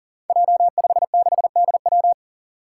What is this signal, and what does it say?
Morse code, heard as a keyed tone, about 40 wpm, 700 Hz. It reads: J56DW
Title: Five Challenging Letter-Number Combinations 40wpm